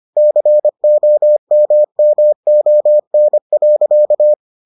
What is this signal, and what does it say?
Morse code, heard as a keyed tone, about 25 wpm, 600 Hz. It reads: COMMON.